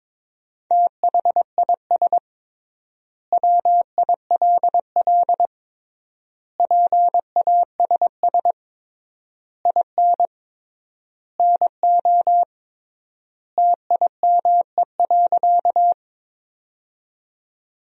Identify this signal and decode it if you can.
Morse code, heard as a keyed tone, about 22 wpm, 700 Hz. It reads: THIS WILL PASS IN NO TIME.
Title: 3x Word-Spacing - Sentences from Top 400 Words - 22wpm